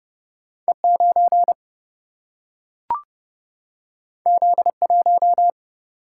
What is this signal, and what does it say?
Morse code, heard as a keyed tone, about 30 wpm, 700 Hz. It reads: E 9 Z 1
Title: Single-Letter Single-Number 30wpm